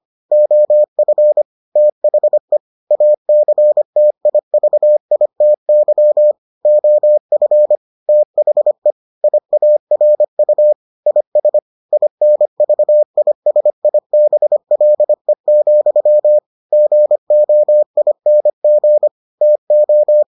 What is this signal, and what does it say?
Morse code, heard as a keyed tone, about 25 wpm, 600 Hz. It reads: OF THE ACTIVITY OF THE IARU IS INVISIBLE, GOING TO